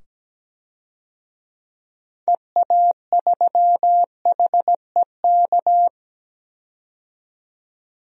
Morse code, heard as a keyed tone, about 17 wpm, 700 Hz. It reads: EA3HEK
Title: DXCC-100 Call Signs - 6 Characters - 17wpm